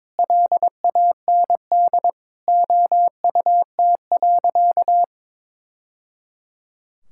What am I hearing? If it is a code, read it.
Morse code, heard as a keyed tone, about 22 wpm, 700 Hz. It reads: LAND OUT.
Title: Sentences from Top 1000 Words - Review Part 5 - 22wpm